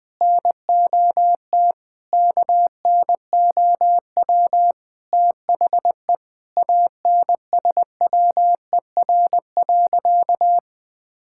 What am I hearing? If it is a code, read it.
Morse code, heard as a keyed tone, about 20 wpm, 700 Hz. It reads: NOT KNOW THE ANSWER.
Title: Sentences with 5-Word Phrases - 20wpm